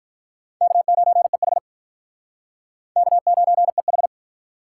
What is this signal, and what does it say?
Morse code, heard as a keyed tone, about 48 wpm, 700 Hz. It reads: K9EH K9EH